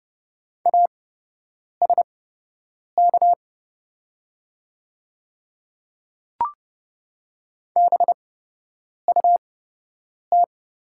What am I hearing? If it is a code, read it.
Morse code, heard as a keyed tone, about 30 wpm, spaced as for 8 wpm, 700 Hz. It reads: ASK BUT
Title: 8x Character Spacing - Top 200 Words - 30wpm